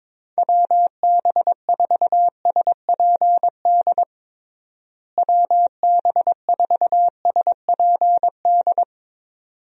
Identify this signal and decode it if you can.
Morse code, heard as a keyed tone, about 22 wpm, 700 Hz. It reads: WB4SPD WB4SPD
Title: USA Call Signs - 6 Characters - Repeated 2x - 22wpm